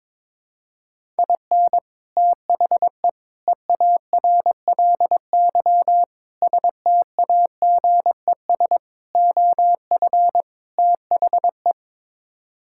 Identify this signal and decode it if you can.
Morse code, heard as a keyed tone, about 21 wpm, 700 Hz. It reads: IN THE EARLY STAGES OF THE
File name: practice / 22wpm / Sets of 6 Words - 22wpm